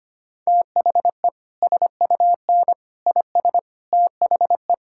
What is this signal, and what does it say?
Morse code, heard as a keyed tone, about 25 wpm, 700 Hz. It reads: THE SUN IS THE
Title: Sentences from Top 1000 Words - Review Part 5 - 25wpm